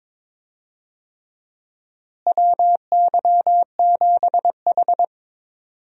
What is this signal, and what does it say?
Morse code, heard as a keyed tone, about 21 wpm, 700 Hz. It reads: WY7H
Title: USA Call Signs - 4 Characters - Repeated 3x - 22wpm